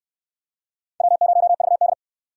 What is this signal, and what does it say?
Morse code, heard as a keyed tone, about 64 wpm, 700 Hz. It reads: W0VD